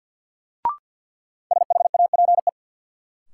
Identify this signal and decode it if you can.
Morse code, heard as a keyed tone, about 51 wpm, 700 Hz. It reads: SHAPE
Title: Sets of 5 Words Encore - New Word Drill - 50wpm